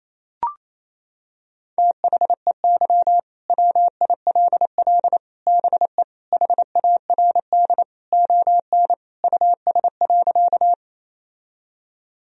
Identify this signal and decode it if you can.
Morse code, heard as a keyed tone, about 28 wpm, 700 Hz. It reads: THEY WILL BE HARD ON US.
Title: Sentences from Top 1000 Words - Review Part 1 - 28wpm